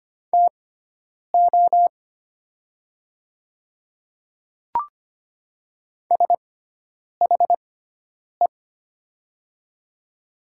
Morse code, heard as a keyed tone, about 25 wpm, spaced as for 9 wpm, 700 Hz. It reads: TO SHE
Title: 6x Character Spacing - Top 100 Words - 25wpm